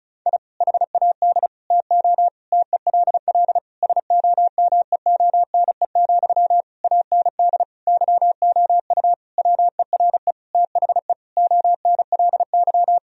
Morse code, heard as a keyed tone, about 34 wpm, 700 Hz. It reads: I HAD TO TELL SOMEONE, AND YOU WERE THE ONLY
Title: Sentences from Top 1000 Words - Review Part 4 - 35wpm